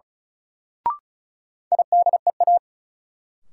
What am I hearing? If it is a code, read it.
Morse code, heard as a keyed tone, about 35 wpm, 700 Hz. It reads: IDEA
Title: Sentences from Top 300 Words - New Word Drill - 35wpm